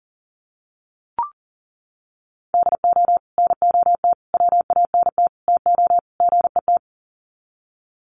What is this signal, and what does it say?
Morse code, heard as a keyed tone, about 40 wpm, 700 Hz. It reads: DO NOT WANT TO GET
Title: Sets of 5 Words 40wpm